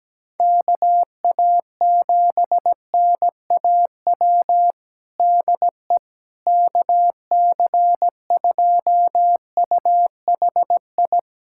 Morse code, heard as a keyed tone, about 17 wpm, 700 Hz. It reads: KA7NAW DE KC2UHI
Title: USA Callsign DE Callsign 17wpm